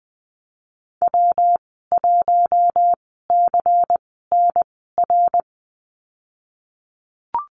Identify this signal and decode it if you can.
Morse code, heard as a keyed tone, about 19 wpm, spaced as for 14 wpm, 700 Hz. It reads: W1CNR
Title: 173 Lesson 28g - Number 9 - Call Signs 20wpm